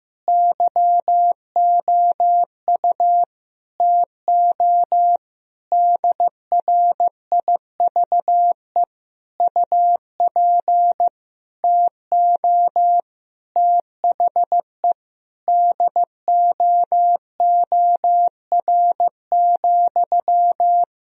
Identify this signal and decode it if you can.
Morse code, heard as a keyed tone, about 15 wpm, 700 Hz. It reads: YOU TO DRIVE UP TO THE DOOR,
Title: Sentences from Top 1000 Words - Review Part 2 - 15wpm